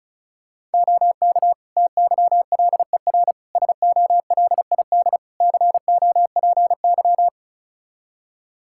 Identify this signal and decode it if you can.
Morse code, heard as a keyed tone, about 34 wpm, 700 Hz. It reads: OK TYLER SOLID COPY
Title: Practice QSO - Exchange 3 - More Info - 35wpm